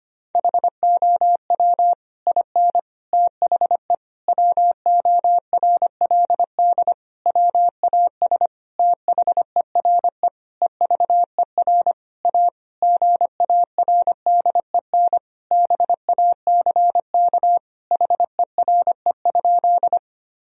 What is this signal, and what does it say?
Morse code, heard as a keyed tone, about 25 wpm, 700 Hz. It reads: HOW IN THE WORLD WAS THERE EVER A GARDEN BACK HERE?
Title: Sentences from Top 1000 Words - Review Part 4 - 25wpm